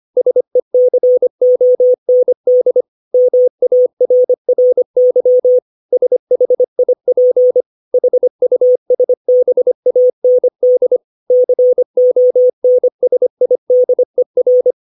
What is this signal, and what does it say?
Morse code, heard as a keyed tone, about 25 wpm, 500 Hz. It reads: SECOND MARRY SHIP HUSBAND CONSIDER
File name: 632 words / 632 Words 25WPM